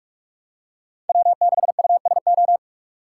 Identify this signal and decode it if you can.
Morse code, heard as a keyed tone, about 46 wpm, 700 Hz. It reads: W6USO
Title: USA Call Signs - 6x Character Spacing - 45wpm